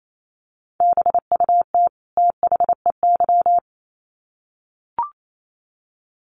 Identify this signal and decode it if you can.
Morse code, heard as a keyed tone, about 28 wpm, 700 Hz. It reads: BUT THEY
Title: Sets of 2 Words 28wpm